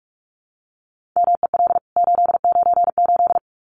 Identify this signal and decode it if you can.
Morse code, heard as a keyed tone, about 44 wpm, 700 Hz. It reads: MEL 898
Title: CWT Exchange 45wpm